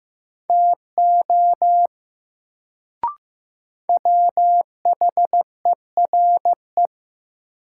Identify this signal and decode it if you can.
Morse code, heard as a keyed tone, about 15 wpm, 700 Hz. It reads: TO WHERE
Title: Sets of 2 Words - New Word Drill - 15wpm